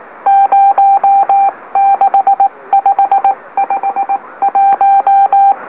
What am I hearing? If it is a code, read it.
Morse code, heard as a keyed tone, about 19 wpm, spaced as for 16 wpm, 790 Hz. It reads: 06551
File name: M04- -2345utc-LOLO